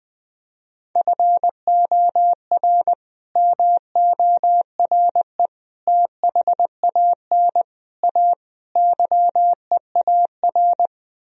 Morse code, heard as a keyed tone, about 21 wpm, 700 Hz. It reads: FOR MORE THAN A YEAR